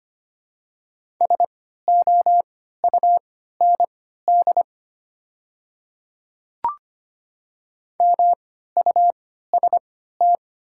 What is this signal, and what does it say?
Morse code, heard as a keyed tone, about 25 wpm, spaced as for 14 wpm, 700 Hz. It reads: SOUND MUST
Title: 3x Character Spacing - Top 200 Words - 25wpm